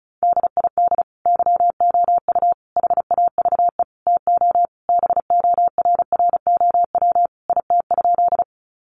Morse code, heard as a keyed tone, about 34 wpm, 700 Hz. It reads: DID YOU HAVE TO BORROW IT?
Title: Sentences from Top 1500 Words 35wpm